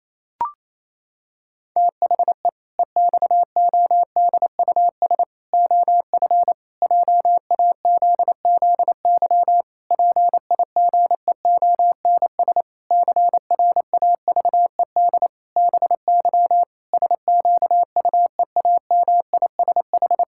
Morse code, heard as a keyed tone, about 28 wpm, 700 Hz. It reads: THE EXODUS OF JAZZY PIGEONS CRAVED BY SQUEAMISH
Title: Pangram Challenge v2 - 28wpm